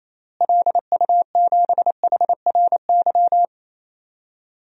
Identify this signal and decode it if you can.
Morse code, heard as a keyed tone, about 28 wpm, 700 Hz. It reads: LU7HRY
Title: DXCC-100 Call Signs - 6 Characters - Repeated 3x - 28wpm